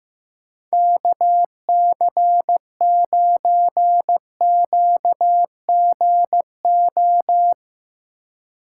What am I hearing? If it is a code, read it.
Morse code, heard as a keyed tone, about 20 wpm, 700 Hz. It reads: KC9QGO